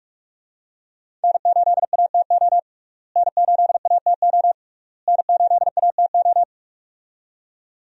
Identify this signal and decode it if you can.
Morse code, heard as a keyed tone, about 44 wpm, 700 Hz. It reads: N8ATO N8ATO N8ATO
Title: USA Call Signs - 5 Characters - Repeated 3x - 45wpm